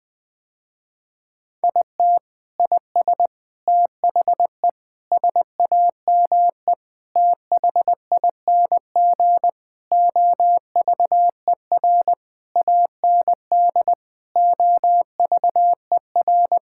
Morse code, heard as a keyed tone, about 20 wpm, 700 Hz. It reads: IT IS THE SAME THING OVER AND OVER
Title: Sentences with 6-Word Phrases - Spoken Beginning - 20wpm